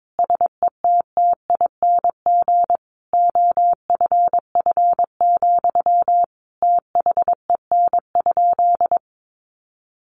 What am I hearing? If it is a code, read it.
Morse code, heard as a keyed tone, about 22 wpm, 700 Hz. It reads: SETTING OFF, THEN?
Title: Sentences from Top 1300 Words 22wpm